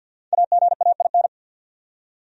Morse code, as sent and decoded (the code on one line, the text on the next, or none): .- --. .- .. -.
AGAIN